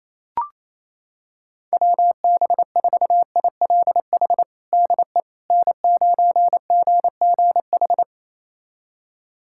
Morse code, heard as a keyed tone, about 28 wpm, 700 Hz. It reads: WB4ILH DE N9GGH
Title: USA Callsign DE Callsign 28wpm